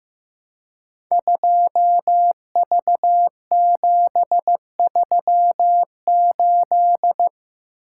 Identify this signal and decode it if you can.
Morse code, heard as a keyed tone, about 15 wpm, 700 Hz. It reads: 2V738
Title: Code Groups - 5 Letter:Numbers - 15wpm